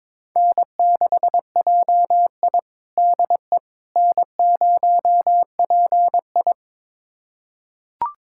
Morse code, heard as a keyed tone, about 22 wpm, 700 Hz. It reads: N6JI DE N0PI
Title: USA Callsign DE Callsign - 4 Characters - 22wpm